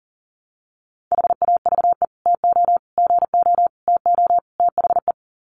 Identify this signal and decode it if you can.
Morse code, heard as a keyed tone, about 40 wpm, 700 Hz. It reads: HAVE TO GO TO THE
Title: Sets of 5 Words Encore 40wpm